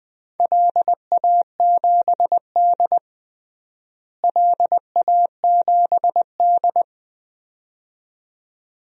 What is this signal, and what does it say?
Morse code, heard as a keyed tone, about 20 wpm, 700 Hz. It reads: LA7D LA7D
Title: DXCC-100 Call Signs - 4 Characters - Repeated 2x - 20wpm